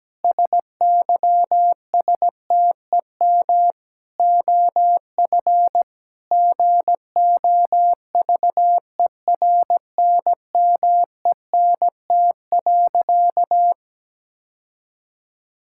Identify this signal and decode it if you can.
Morse code, heard as a keyed tone, about 17 wpm, 700 Hz. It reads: SYSTEM OF GOVERNMENT.
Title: Sentences from Top 1000 Words - Review Part 6 - 17wpm